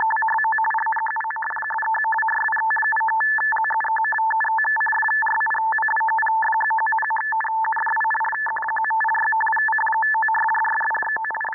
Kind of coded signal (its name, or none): two-tone teleprinter signal